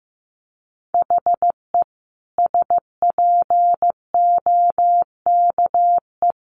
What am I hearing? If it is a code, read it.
Morse code, heard as a keyed tone, about 15 wpm, 700 Hz. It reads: HE SPOKE
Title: Sentences from Top 1200 Words 15wpm